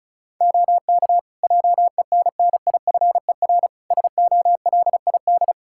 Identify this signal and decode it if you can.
Morse code, heard as a keyed tone, about 36 wpm, 700 Hz. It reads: OK JENNIFER SOLID
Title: Practice QSO - Exchange 3 - More Info - 35wpm